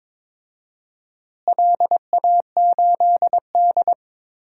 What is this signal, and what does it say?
Morse code, heard as a keyed tone, about 21 wpm, 700 Hz. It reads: LA8D
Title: DXCC-100 Call Signs - 4 Characters - Repeated 2x - 22wpm